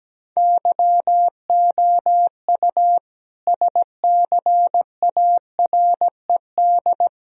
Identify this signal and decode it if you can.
Morse code, heard as a keyed tone, about 17 wpm, 700 Hz. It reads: YOU SCARED
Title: Sentences from Top 1000 Words - Review Part 4 - 17wpm